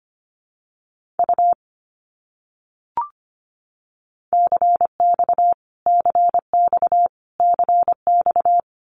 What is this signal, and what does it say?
Morse code, heard as a keyed tone, about 25 wpm, 700 Hz. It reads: U CX CX CX
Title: Abbreviations Repeated 3x - 25wpm